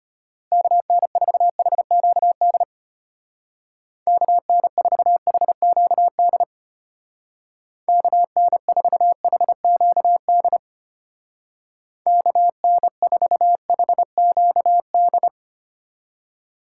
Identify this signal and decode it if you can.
Morse code, heard as a keyed tone, about 30 wpm, 700 Hz. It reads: KN4HQD KN4HQD KN4HQD KN4HQD
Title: Speed Racing - USA 2x3 Call Signs 25wpm